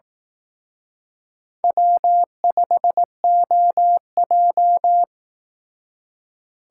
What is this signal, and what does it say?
Morse code, heard as a keyed tone, about 18 wpm, 700 Hz. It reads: W5OJ
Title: Speed Racing - USA 1x2 Call Signs 15wpm